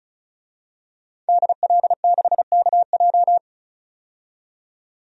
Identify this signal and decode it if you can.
Morse code, heard as a keyed tone, about 36 wpm, 700 Hz. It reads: DL6KJ
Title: DXCC-100 Call Signs - 5 Characters - 35wpm